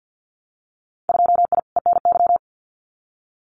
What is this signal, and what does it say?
Morse code, heard as a keyed tone, about 50 wpm, 700 Hz. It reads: 2I ENY
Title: Winter Field Day Exchanges Encore 50wpm